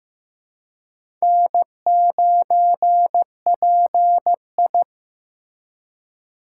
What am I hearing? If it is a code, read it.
Morse code, heard as a keyed tone, about 15 wpm, 700 Hz. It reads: N9PI
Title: USA Call Signs - 4 Characters - 3x Character Spacing - 15wpm